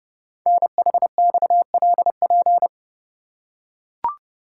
Morse code, heard as a keyed tone, about 29 wpm, 700 Hz. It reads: NHXLP
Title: Code Groups - 5 Letter:Numbers - 30wpm